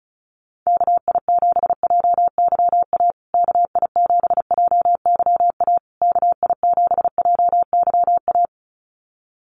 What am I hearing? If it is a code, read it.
Morse code, heard as a keyed tone, about 34 wpm, 700 Hz. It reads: KI7JYA KI7JYA KI7JYA
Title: USA Call Signs Repeated 3x 35wpm